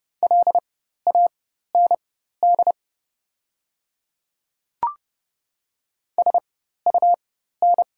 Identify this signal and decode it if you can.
Morse code, heard as a keyed tone, about 30 wpm, spaced as for 14 wpm, 690 Hz. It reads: LAND SUN
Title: 4x Character Spacing - Top 300 Words - 30wpm